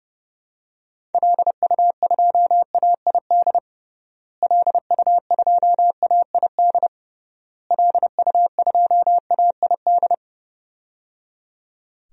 Morse code, heard as a keyed tone, about 30 wpm, 700 Hz. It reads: LU2AID LU2AID LU2AID
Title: DXCC-100 Call Signs - 6 Characters - Repeated 3x - 30wpm